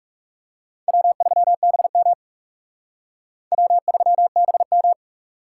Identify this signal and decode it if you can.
Morse code, heard as a keyed tone, about 37 wpm, 700 Hz. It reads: W3BM W3BM
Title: Speed Racing - USA 1x2 Call Signs 30wpm